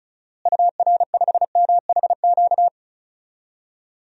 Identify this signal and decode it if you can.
Morse code, heard as a keyed tone, about 36 wpm, 700 Hz. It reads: UR5MHQ
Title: DXCC-100 Call Signs - 6 Characters - Repeated 3x - 35wpm